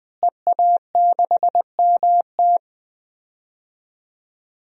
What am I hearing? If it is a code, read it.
Morse code, heard as a keyed tone, about 21 wpm, 700 Hz. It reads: EA6MT
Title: DXCC-100 Call Signs - 5 Characters - Repeated 2x - 20wpm